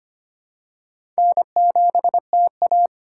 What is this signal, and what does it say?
Morse code, heard as a keyed tone, about 25 wpm, 700 Hz. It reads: N7TA
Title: USA Call Signs - 4 Characters - 2x Character Spacing - 25wpm